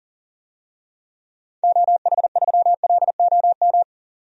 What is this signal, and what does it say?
Morse code, heard as a keyed tone, about 37 wpm, 700 Hz. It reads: OH3LOM
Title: DXCC-100 Call Signs - 6 Characters - 40wpm